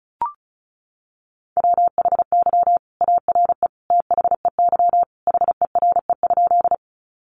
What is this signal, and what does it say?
Morse code, heard as a keyed tone, about 34 wpm, 700 Hz. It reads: WHY ARE THEY HERE?
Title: Sentences from Top 200 Words 35wpm